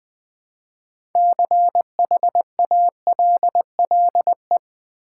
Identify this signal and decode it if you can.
Morse code, heard as a keyed tone, about 20 wpm, 700 Hz. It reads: CHALLE
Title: Build A Word - 3 Letters At A Time - 9-Letter Words - 20wpm